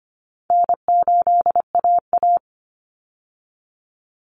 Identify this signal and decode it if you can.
Morse code, heard as a keyed tone, about 25 wpm, 700 Hz. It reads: N8AA
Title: Easy Call Signs Repeated 2x - 25wpm